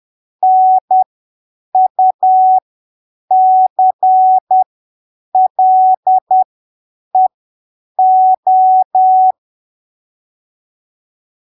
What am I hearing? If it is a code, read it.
Morse code, heard as a keyed tone, about 10 wpm, spaced as for 7 wpm, 750 Hz. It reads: NUCLEO
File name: arquivos / texto / texto 11 10ppm